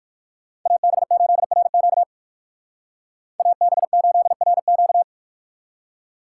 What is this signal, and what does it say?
Morse code, heard as a keyed tone, about 65 wpm, spaced as for 57 wpm, 700 Hz. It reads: AB8RQ AB8RQ